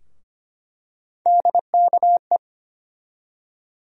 Morse code, heard as a keyed tone, about 25 wpm, 700 Hz. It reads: DKE
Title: USA Call Sign Postfixes - 3 Letters 25wpm